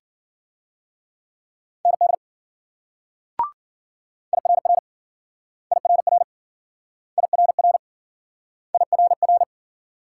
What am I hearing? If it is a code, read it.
Morse code, heard as a keyed tone, about 52 wpm, 700 Hz. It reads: ND IRR IRR IRR IRR